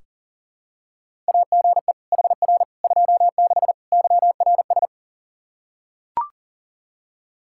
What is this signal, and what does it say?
Morse code, heard as a keyed tone, about 40 wpm, 700 Hz. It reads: AGE HR 26 YRS
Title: Practice QSO - Exchange 3 - More Info - 40wpm